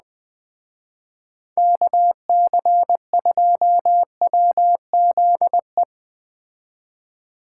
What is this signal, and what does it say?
Morse code, heard as a keyed tone, about 20 wpm, 700 Hz. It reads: KC2WZE